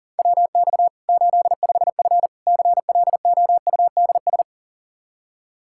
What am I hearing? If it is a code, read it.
Morse code, heard as a keyed tone, about 40 wpm, 700 Hz. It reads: WX 85F CLOUDS
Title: Practice QSO - Exchange 2 - Info - 40wpm